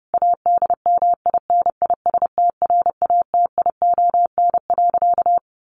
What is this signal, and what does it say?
Morse code, heard as a keyed tone, about 28 wpm, 700 Hz. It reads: ADMINISTRATION.
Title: Sentences from Top 1300 Words 30wpm